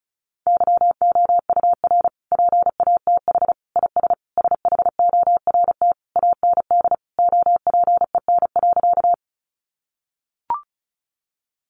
Morse code, heard as a keyed tone, about 34 wpm, 700 Hz. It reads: YOUR PATH IS SHORT AND OPEN.
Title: Sentences from Top 1400 Words 35wpm